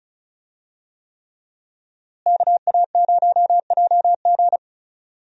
Morse code, heard as a keyed tone, about 35 wpm, 700 Hz. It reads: KA0JG
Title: USA Call Signs - 5 Characters - Repeated 3x - 35wpm